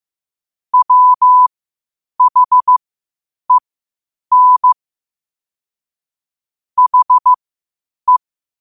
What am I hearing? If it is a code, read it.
Morse code, heard as a keyed tone, about 15 wpm, spaced as for 8 wpm, 1000 Hz. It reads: WHEN HE
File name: qotdfiles / 8wpm